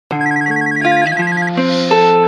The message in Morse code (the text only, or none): none